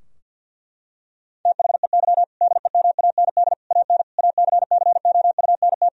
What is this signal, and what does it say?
Morse code, heard as a keyed tone, about 50 wpm, 700 Hz. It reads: THEY DEMAND AN ACCOUNT
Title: Sentences from Top 1000 Words - Review Part 4 - 50wpm